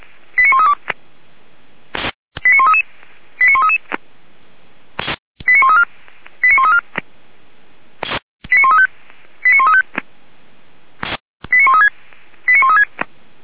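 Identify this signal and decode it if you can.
Five-tone selective call (ZVEI), heard as quick runs of five tones, tones of about 70 ms. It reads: 89132 89133 89133 89134 89134 89135 89135 89136 89136
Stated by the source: Received on SDR as ZVEI five-tone selective call 89131.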